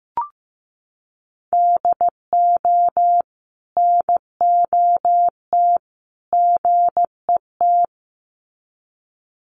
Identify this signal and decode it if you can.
Morse code, heard as a keyed tone, about 15 wpm, 700 Hz. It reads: DO NOT GET
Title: Sets of 3 Words 15wpm